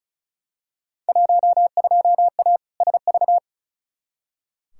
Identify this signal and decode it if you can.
Morse code, heard as a keyed tone, about 36 wpm, 700 Hz. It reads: 12A SV
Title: Field Day Exchanges Encore - 35wpm